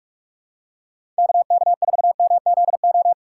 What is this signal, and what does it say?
Morse code, heard as a keyed tone, about 46 wpm, 700 Hz. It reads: KK4MZO
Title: USA Call Signs - 6 Characters - 4x Character Spacing - 45wpm